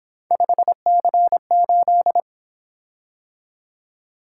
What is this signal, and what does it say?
Morse code, heard as a keyed tone, about 26 wpm, 700 Hz. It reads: 5C8
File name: Speed-Racing - Three Letter-Number Combinations - 22wpm